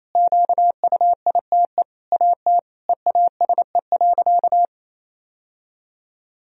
Morse code, heard as a keyed tone, about 28 wpm, 700 Hz. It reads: QUITE AT EASE.
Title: Sentences from Top 1000 Words - Review Part 1 - 28wpm